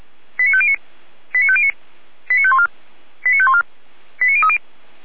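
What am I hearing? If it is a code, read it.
Five-tone selective call (ZVEI), heard as quick runs of five tones, tones of about 70 ms. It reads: 89509 89509 89514 89514 89030
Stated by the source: Five-tone ZVEI selective call, SDR received, from feuerwehr-orlamuende.org fire brigade audio archive.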